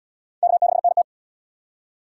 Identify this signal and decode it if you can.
Morse code, heard as a keyed tone, about 77 wpm, 700 Hz. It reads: K6NT